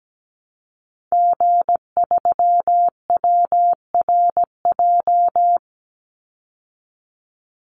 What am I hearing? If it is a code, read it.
Morse code, heard as a keyed tone, about 17 wpm, 700 Hz. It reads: G3WRJ
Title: Call Signs Repeated 3x - 17wpm